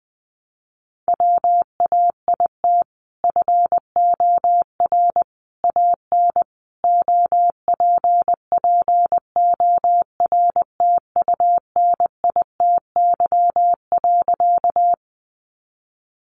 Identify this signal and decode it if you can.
Morse code, heard as a keyed tone, about 20 wpm, 700 Hz. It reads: WAIT FOR AN OPPORTUNITY.
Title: Sentences from Top 1000 Words 20wpm